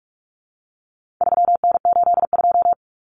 Morse code, heard as a keyed tone, about 44 wpm, 700 Hz. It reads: 3N82
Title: Common Cut Numbers - 4-Digit 45wpm